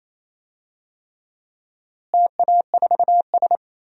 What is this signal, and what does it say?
Morse code, heard as a keyed tone, about 28 wpm, 700 Hz. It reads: TA4S
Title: DXCC-100 Call Signs - 4 Characters - Repeated 3x - 28wpm